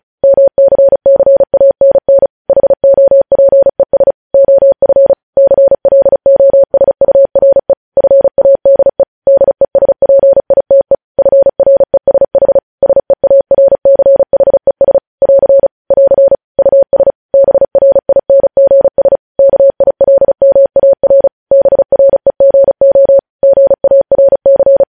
Morse code, heard as a keyed tone, about 36 wpm, 550 Hz. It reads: MCCANN HOPES OF CLOSURE FADE DESPITE FRESH SEARCHES <AR> <AR> US BRINGS KILMAR BREGO GARC
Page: News Headlines in Morse Code at 35 WPM